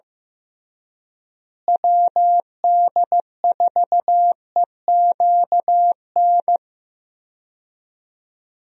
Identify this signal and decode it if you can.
Morse code, heard as a keyed tone, about 16 wpm, 700 Hz. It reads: WD4EQN
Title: USA Call Signs - 6 Characters - Repeated 3x - 15wpm